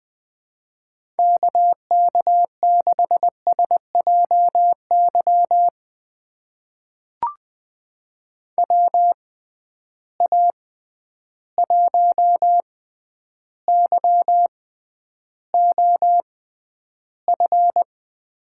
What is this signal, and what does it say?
Morse code, heard as a keyed tone, about 20 wpm, spaced as for 10 wpm, 700 Hz. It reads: KK6SJY WA1YOF
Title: USA Call Signs - 6 Characters - 6x Character Spacing - 20wpm